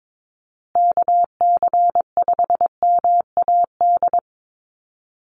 Morse code, heard as a keyed tone, about 22 wpm, 700 Hz. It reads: KC5MAD
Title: USA Call Signs Repeated 3x 22wpm